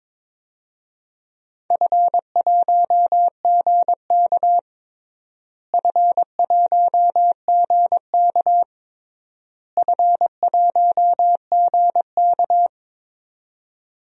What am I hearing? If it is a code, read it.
Morse code, heard as a keyed tone, about 22 wpm, 700 Hz. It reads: F1GK F1GK F1GK
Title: DXCC-100 Call Signs - 4 Characters - Repeated 3x - 22wpm